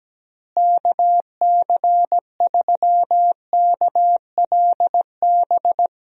Morse code, heard as a keyed tone, about 17 wpm, 700 Hz. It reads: KC3KLB
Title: USA Call Signs - 6 Characters - Repeated 3x - 17wpm